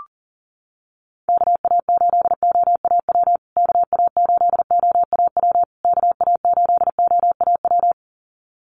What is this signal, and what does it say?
Morse code, heard as a keyed tone, about 40 wpm, 700 Hz. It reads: KA8OAW KA8OAW KA8OAW
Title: USA Call Signs Repeated 3x 40wpm